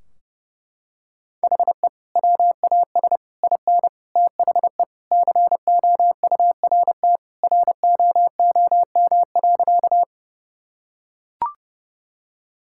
Morse code, heard as a keyed tone, about 32 wpm, 700 Hz. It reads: HE WAS IN THE COURT ROOM.
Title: Sentences from Top 1600 Words 30wpm